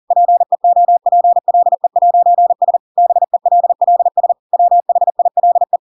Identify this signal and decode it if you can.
Morse code, heard as a keyed tone, about 40 wpm, 700 Hz. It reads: PEOPLE'S BELLS WHILE
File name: Pinocchio / Pinocchio ch6 40 40 ik1hge